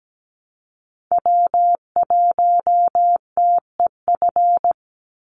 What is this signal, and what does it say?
Morse code, heard as a keyed tone, about 17 wpm, 700 Hz. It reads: W1TEF
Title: Call Signs Repeated 2x - 17wpm